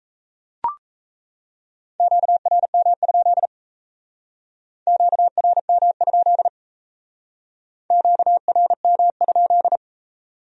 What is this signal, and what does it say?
Morse code, heard as a keyed tone, about 35 wpm, 700 Hz. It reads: QRM? QRM? QRM?